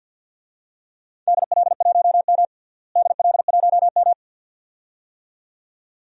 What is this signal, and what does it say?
Morse code, heard as a keyed tone, about 52 wpm, 700 Hz. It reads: DL1M DL1M
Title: DXCC-100 Call Signs - 4 Characters - Repeated 2x - 50wpm